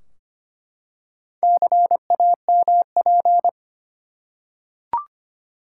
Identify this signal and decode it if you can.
Morse code, heard as a keyed tone, about 25 wpm, 700 Hz. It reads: CAMP
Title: Four Letter Words - 25wpm